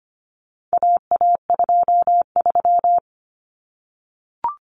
Morse code, heard as a keyed tone, about 25 wpm, 700 Hz. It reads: AA23
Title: Common Cut Numbers - 4-Digit 25wpm